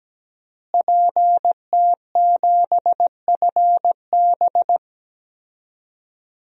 Morse code, heard as a keyed tone, about 17 wpm, 700 Hz. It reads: PT7FB
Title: DXCC-100 Call Signs - 5 Characters - Repeated 2x - 17wpm